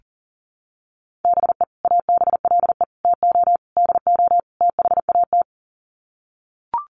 Morse code, heard as a keyed tone, about 40 wpm, 700 Hz. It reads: BE ABLE TO DO THAT
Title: Sets of 5 Words Encore2 40wpm